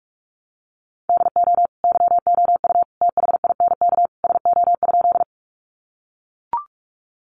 Morse code, heard as a keyed tone, about 44 wpm, 700 Hz. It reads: DO YOU THINK SO?
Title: Sentences from Top 200 Words 45wpm